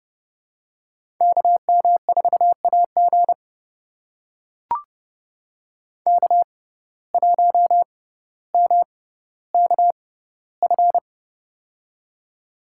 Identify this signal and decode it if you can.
Morse code, heard as a keyed tone, about 30 wpm, spaced as for 15 wpm, 700 Hz. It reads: KM4AG K1MKF
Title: USA Call Signs - 5 Characters - 6x Character Spacing - 30wpm